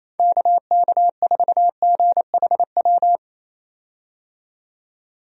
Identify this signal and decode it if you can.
Morse code, heard as a keyed tone, about 28 wpm, 700 Hz. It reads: KK4GHW
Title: USA Call Signs - 6 Characters - 28wpm